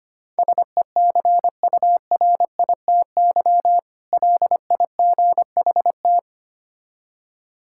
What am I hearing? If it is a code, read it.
Morse code, heard as a keyed tone, about 26 wpm, 700 Hz. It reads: SECURITY LIGHT
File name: practice / 25wpm / Difficult Words - Speed-Racing with Hints - 25wpm